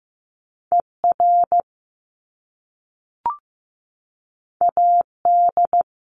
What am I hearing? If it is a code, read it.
Morse code, heard as a keyed tone, about 15 wpm, 700 Hz. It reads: ER AD
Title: Top Two Letter Combinations in Words 15wpm